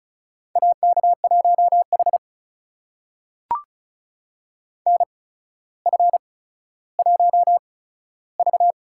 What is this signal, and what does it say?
Morse code, heard as a keyed tone, about 36 wpm, spaced as for 15 wpm, 700 Hz. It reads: AK1H NF1V
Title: USA Call Signs - 4 Characters - 8x Character Spacing - 35wpm